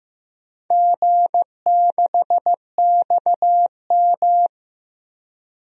Morse code, heard as a keyed tone, about 15 wpm, 700 Hz. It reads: G6XM
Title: DXCC-100 Call Signs - 4 Characters - Repeated 3x - 15wpm